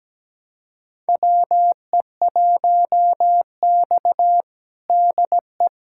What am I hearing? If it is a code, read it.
Morse code, heard as a keyed tone, about 17 wpm, 700 Hz. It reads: WE1X DE
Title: USA Callsign DE Callsign - 4 Characters - 17wpm